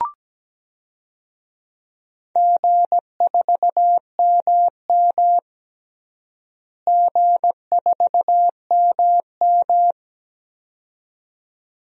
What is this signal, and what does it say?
Morse code, heard as a keyed tone, about 17 wpm, 700 Hz. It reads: G4MM G4MM
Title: DXCC-100 Call Signs - 4 Characters - Repeated 2x - 17wpm